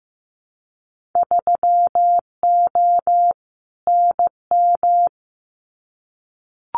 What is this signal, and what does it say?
Morse code, heard as a keyed tone, about 15 wpm, 700 Hz. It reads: 3O NM
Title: Winter Field Day Exchanges Encore 15wpm